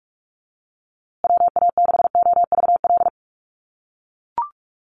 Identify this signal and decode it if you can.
Morse code, heard as a keyed tone, about 44 wpm, 700 Hz. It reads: WA6OVL
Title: USA Call Signs Repeated 2x 45wpm